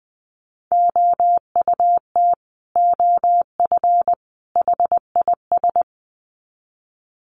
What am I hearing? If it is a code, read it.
Morse code, heard as a keyed tone, about 20 wpm, 700 Hz. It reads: OUT OF HIS
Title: Sets of 3 Words 20wpm